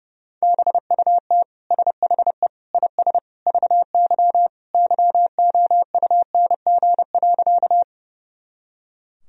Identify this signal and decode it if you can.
Morse code, heard as a keyed tone, about 30 wpm, 700 Hz. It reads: BUT SHE IS VY YOUNG.
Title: Sentences from Top 400 Words and Abbreviations 30wpm